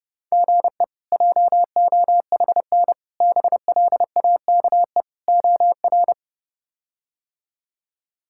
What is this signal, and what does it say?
Morse code, heard as a keyed tone, about 30 wpm, 700 Hz. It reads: GE JOHN BLAKE OR
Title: K1USN SST QSO Practice - 30wpm